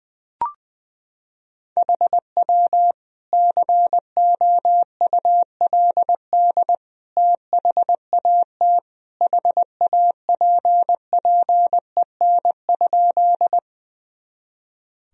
Morse code, heard as a keyed tone, about 20 wpm, 700 Hz. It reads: HW COULD THAT HAPPEN?
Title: Sentences from Top 400 Words and Abbreviations 20wpm